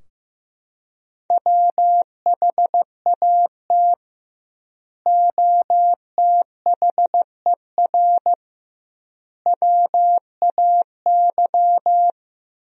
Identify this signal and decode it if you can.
Morse code, heard as a keyed tone, about 16 wpm, 700 Hz. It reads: WHAT OTHER WAY
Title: 2x Word-Spacing - Sentences from Top 100 Words - 15wpm